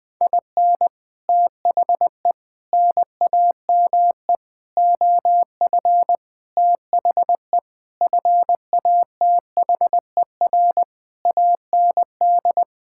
Morse code, heard as a keyed tone, about 20 wpm, 700 Hz. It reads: IN THE NAME OF THE FATHER AND
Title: Sentences with 9-Word Phrases - Spoken Beginning - 20wpm